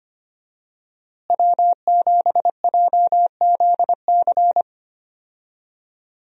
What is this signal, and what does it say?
Morse code, heard as a keyed tone, about 25 wpm, 700 Hz. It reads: W7JZC
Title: USA Call Signs - 5 Characters - Repeated 2x - 25wpm